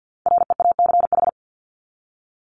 Morse code, heard as a keyed tone, about 49 wpm, 700 Hz. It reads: REACH
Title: ED Suffix - Top 100 Words with Base Words 50wpm